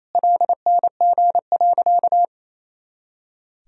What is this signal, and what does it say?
Morse code, heard as a keyed tone, about 28 wpm, 700 Hz. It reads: LNG.
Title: Sentences from Top 100 Words and Abbreviations 28wpm